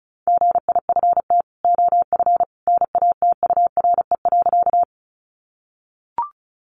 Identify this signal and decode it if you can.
Morse code, heard as a keyed tone, about 34 wpm, 700 Hz. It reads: GIFT OF NATURE.
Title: Sentences from Top 900 Words 35wpm